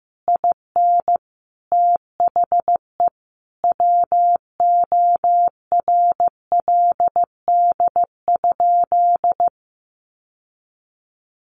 Morse code, heard as a keyed tone, about 15 wpm, 700 Hz. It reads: IN THE WORLD?
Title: Sentences from Top 1400 Words 15wpm